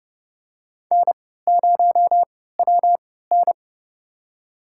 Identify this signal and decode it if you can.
Morse code, heard as a keyed tone, about 32 wpm, spaced as for 17 wpm, 700 Hz. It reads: N0WN
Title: USA Call Signs - 4 Characters - 3x Character Spacing - 30wpm